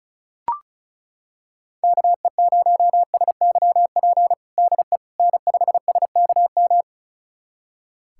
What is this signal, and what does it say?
Morse code, heard as a keyed tone, about 36 wpm, 700 Hz. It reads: KE0SYP DE N5SKM
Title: USA Callsign DE Callsign 35wpm